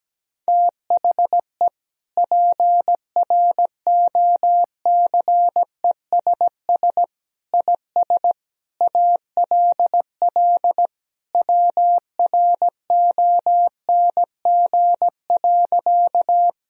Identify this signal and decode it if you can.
Morse code, heard as a keyed tone, about 17 wpm, 700 Hz. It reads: THE PROCESS IS ALL WRONG.
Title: Sentences from Top 1000 Words - Review Part 3 - 17wpm